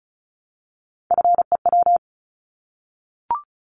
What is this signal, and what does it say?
Morse code, heard as a keyed tone, about 34 wpm, 700 Hz. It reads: FEW
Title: Multi-letter Phonemes Repeated 3x 35wpm